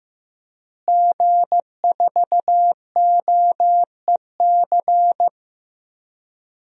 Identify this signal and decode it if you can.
Morse code, heard as a keyed tone, about 15 wpm, 700 Hz. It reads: G4OEC
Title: DXCC-100 Call Signs - 5 Characters - Repeated 2x - 15wpm